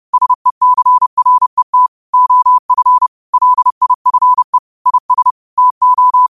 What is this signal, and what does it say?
Morse code, heard as a keyed tone, about 30 wpm, 1000 Hz. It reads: SECRET OF LIFE IS TO
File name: qotdfiles / 30wpm